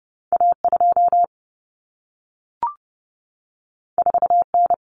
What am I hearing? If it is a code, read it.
Morse code, heard as a keyed tone, about 30 wpm, 700 Hz. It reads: A2 4N
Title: Common Cut Numbers - 2-Digit 30wpm